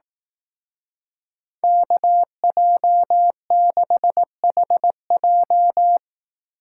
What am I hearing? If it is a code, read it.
Morse code, heard as a keyed tone, about 18 wpm, 700 Hz. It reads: KJ6HJ
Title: Speed Racing - USA 2x2 Call Signs 15wpm